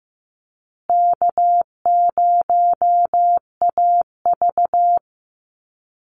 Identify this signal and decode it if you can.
Morse code, heard as a keyed tone, about 15 wpm, 700 Hz. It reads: K0AV
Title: Easy Call Signs Repeated 3x - 15wpm